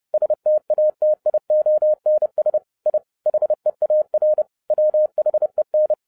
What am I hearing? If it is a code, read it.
Morse code, heard as a keyed tone, about 30 wpm, 610 Hz. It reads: STATIONS I HEAR WHEN